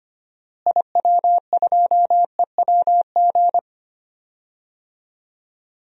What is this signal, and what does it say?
Morse code, heard as a keyed tone, about 25 wpm, 700 Hz. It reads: IW2EWG
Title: DXCC-100 Call Signs - 6 Characters - Repeated 3x - 25wpm